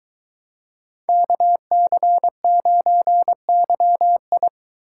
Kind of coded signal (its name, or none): Morse code